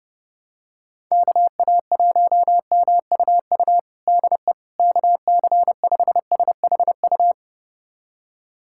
Morse code, heard as a keyed tone, about 30 wpm, 700 Hz. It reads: KA1MUU DE KC5SHU
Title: USA Callsign DE Callsign - 6 Characters - 30wpm